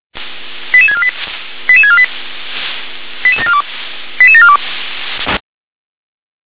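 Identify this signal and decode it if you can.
Five-tone selective call (ZVEI), heard as quick runs of five tones, tones of about 70 ms. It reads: 88648 88648 88742 88742
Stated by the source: ZVEI five-tone selective call received by SDR.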